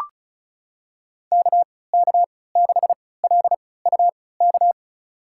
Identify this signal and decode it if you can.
Morse code, heard as a keyed tone, about 36 wpm, spaced as for 20 wpm, 700 Hz. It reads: KK6LUK
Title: USA Call Signs - 6 Characters - 3x Character Spacing - 35wpm